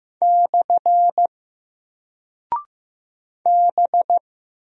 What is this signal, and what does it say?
Morse code, heard as a keyed tone, about 15 wpm, 700 Hz. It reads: / B
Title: Single Character (Letter:Number:Punctuation:Prosign) 15wpm